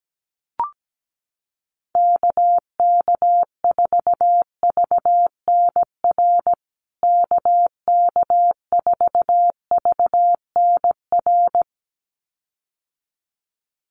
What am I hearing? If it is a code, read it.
Morse code, heard as a keyed tone, about 17 wpm, 700 Hz. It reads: KK4VNR KK4VNR
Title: USA Call Signs Repeated 2x 17wpm